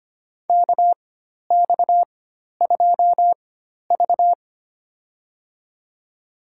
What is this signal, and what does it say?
Morse code, heard as a keyed tone, about 25 wpm, spaced as for 12 wpm, 700 Hz. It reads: KX2V
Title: USA Call Signs - 4 Characters - 4x Character Spacing - 25wpm